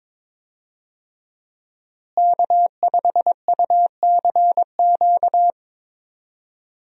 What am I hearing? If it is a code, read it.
Morse code, heard as a keyed tone, about 21 wpm, 700 Hz. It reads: K5UCQ
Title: USA Call Signs - 5 Characters - Repeated 3x - 22wpm